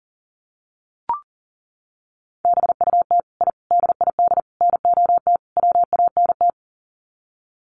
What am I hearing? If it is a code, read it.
Morse code, heard as a keyed tone, about 40 wpm, 700 Hz. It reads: BUT I DID NOT WANT
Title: Sets of 5 Words Encore 40wpm